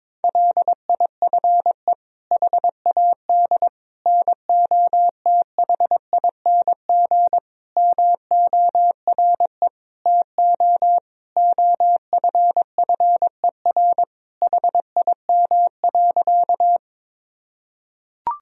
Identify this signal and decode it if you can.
Morse code, heard as a keyed tone, about 22 wpm, 700 Hz. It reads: LIFE HAD NOTHING MORE TO OFFER HIM.
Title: Sentences from Top 1000 Words - Review Part 4 - 22wpm